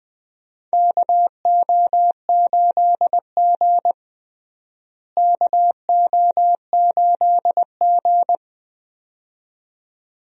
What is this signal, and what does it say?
Morse code, heard as a keyed tone, about 20 wpm, 700 Hz. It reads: KO8G KO8G
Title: USA Call Signs - 4 Characters - Repeated 2x - 20wpm